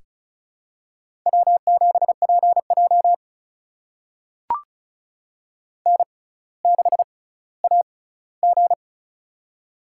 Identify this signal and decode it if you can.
Morse code, heard as a keyed tone, about 36 wpm, spaced as for 18 wpm, 700 Hz. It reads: W7PJ N6AG
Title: USA Call Signs - 4 Characters - 6x Character Spacing - 35wpm